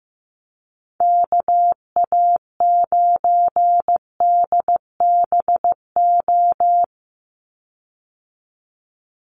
Morse code, heard as a keyed tone, about 15 wpm, 700 Hz. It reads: KA9DBO
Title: USA Call Signs 15wpm